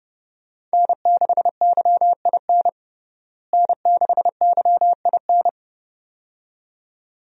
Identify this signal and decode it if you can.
Morse code, heard as a keyed tone, about 30 wpm, 700 Hz. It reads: N6YIN N6YIN
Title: USA Call Signs - 5 Characters - Repeated 2x - 30wpm